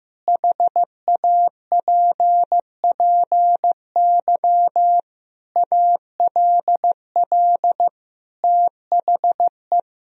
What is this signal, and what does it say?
Morse code, heard as a keyed tone, about 15 wpm, 700 Hz. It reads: HAPPY ALL THE
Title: Sentences from Top 1000 Words - Review Part 4 - 15wpm